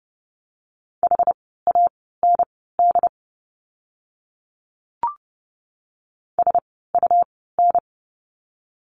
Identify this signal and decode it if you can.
Morse code, heard as a keyed tone, about 30 wpm, spaced as for 17 wpm, 700 Hz. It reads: HAND SUN
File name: Lesson 17f - Letter u - Words 30wpm